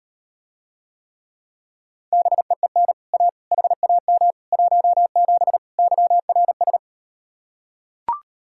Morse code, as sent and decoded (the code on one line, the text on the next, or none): -... . -. / .- / .... .- -- / .---- --... / -.-- .-. ...
BEEN A HAM 17 YRS